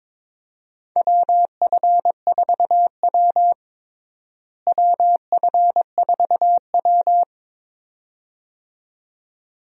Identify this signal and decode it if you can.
Morse code, heard as a keyed tone, about 22 wpm, 700 Hz. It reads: WF4W WF4W
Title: USA Call Signs - 4 Characters - Repeated 2x - 22wpm